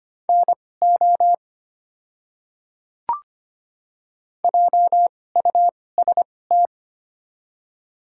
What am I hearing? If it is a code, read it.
Morse code, heard as a keyed tone, about 25 wpm, spaced as for 18 wpm, 700 Hz. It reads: NO JUST
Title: 2x Character Spacing - Top 300 Words - 25wpm